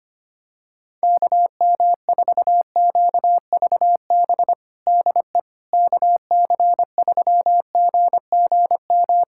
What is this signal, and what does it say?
Morse code, heard as a keyed tone, about 25 wpm, 700 Hz. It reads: KM4QVB DE KC3GGM
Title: USA Callsign DE Callsign - 6 Characters - 25wpm